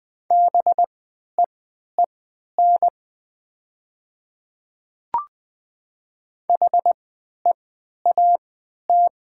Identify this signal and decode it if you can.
Morse code, heard as a keyed tone, about 20 wpm, spaced as for 11 wpm, 700 Hz. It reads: BEEN HEAT
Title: 3x Character Spacing - Top 500 Words - 20wpm